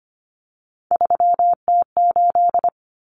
Morse code, heard as a keyed tone, about 25 wpm, 700 Hz. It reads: 3T8
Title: Common Cut Numbers - 3-Digit 25wpm